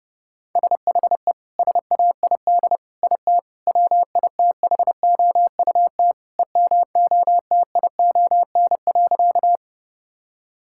Morse code, heard as a keyed tone, about 30 wpm, 700 Hz. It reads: SHE SAID IT WITHOUT EMOTION.
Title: Sentences from Top 1600 Words 30wpm